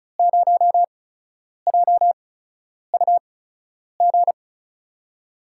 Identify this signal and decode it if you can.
Morse code, heard as a keyed tone, about 36 wpm, spaced as for 10 wpm, 700 Hz. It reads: 0JUG
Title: USA Call Signs - 6 Characters - 8x Character Spacing - 35wpm